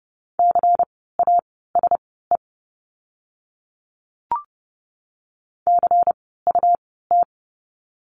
Morse code, heard as a keyed tone, about 30 wpm, spaced as for 17 wpm, 700 Hz. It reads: CASE CUT
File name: Lesson 17f - Letter u - Words 30wpm